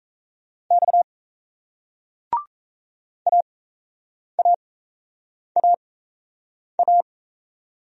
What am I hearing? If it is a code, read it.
Morse code, heard as a keyed tone, about 38 wpm, spaced as for 8 wpm, 700 Hz. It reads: X AAAA